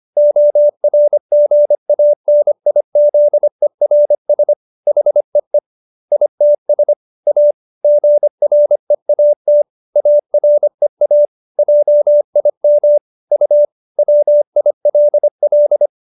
Morse code, heard as a keyed tone, about 25 wpm, 590 Hz. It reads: ORGANIZERS HEE ITS A GREAT AREA JIM U WILL